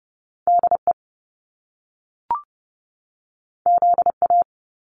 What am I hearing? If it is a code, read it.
Morse code, heard as a keyed tone, about 30 wpm, 700 Hz. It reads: DE ZA
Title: Top Two Letter Combinations in Words 30wpm